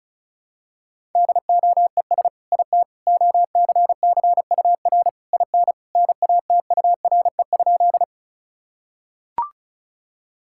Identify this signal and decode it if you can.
Morse code, heard as a keyed tone, about 34 wpm, 700 Hz. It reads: DOES IT OCCUR IN NATURE?
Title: Sentences from Top 1000 Words - Review Part 6 - 35wpm